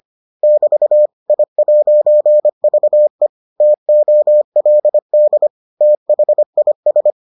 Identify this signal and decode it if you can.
Morse code, heard as a keyed tone, about 25 wpm, 600 Hz. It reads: <BT> I'VE TOLD THIS